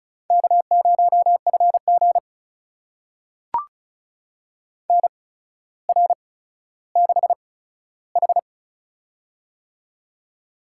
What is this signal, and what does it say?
Morse code, heard as a keyed tone, about 36 wpm, spaced as for 15 wpm, 700 Hz. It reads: K0FG NR6H
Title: USA Call Signs - 4 Characters - 8x Character Spacing - 35wpm